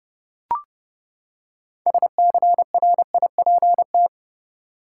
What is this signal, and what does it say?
Morse code, heard as a keyed tone, about 30 wpm, 700 Hz. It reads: SCRIPT
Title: Six Letter Words - 30wpm